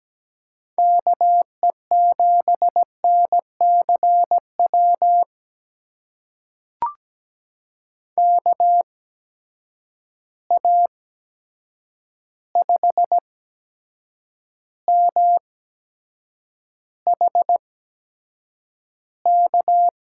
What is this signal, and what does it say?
Morse code, heard as a keyed tone, about 17 wpm, spaced as for 7 wpm, 700 Hz. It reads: KE7NCW KA5MHK
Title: USA Call Signs - 6 Characters - 8x Character Spacing - 17wpm